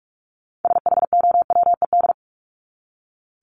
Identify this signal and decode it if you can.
Morse code, heard as a keyed tone, about 44 wpm, 700 Hz. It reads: SHOWED
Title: ED Suffix - Top 100 Words with Base Words 45wpm